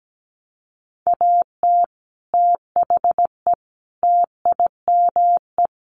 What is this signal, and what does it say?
Morse code, heard as a keyed tone, about 17 wpm, 700 Hz. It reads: AT THE TIME
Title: Sets of 5 Words Encore2 17wpm